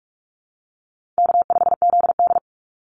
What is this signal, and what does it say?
Morse code, heard as a keyed tone, about 44 wpm, 700 Hz. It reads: K5ZD
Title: Easy Call Signs Repeated 2x - 45wpm